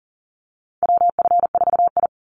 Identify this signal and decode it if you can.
Morse code, heard as a keyed tone, about 40 wpm, 700 Hz. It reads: WF4I
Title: NA SOTA Chaser Callsigns 40wpm